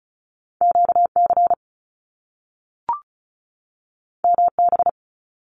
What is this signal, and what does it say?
Morse code, heard as a keyed tone, about 36 wpm, 700 Hz. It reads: QC MB
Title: Canadian Provinces and Territories 35wpm